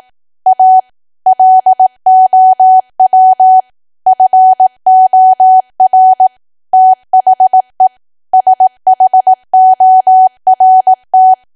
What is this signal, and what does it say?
Morse code, heard as a keyed tone, about 18 wpm, 750 Hz. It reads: A LOW FOR THE SHORT
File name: cwa / W1AW-Level2 / 18WPM